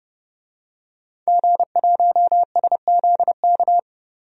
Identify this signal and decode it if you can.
Morse code, heard as a keyed tone, about 30 wpm, 700 Hz. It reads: G1SZK
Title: DXCC-100 Call Signs - 5 Characters - 30wpm